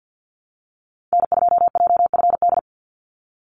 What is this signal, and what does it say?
Morse code, heard as a keyed tone, about 50 wpm, 700 Hz. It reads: N2JFD
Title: Call Signs Repeated 2x - 50wpm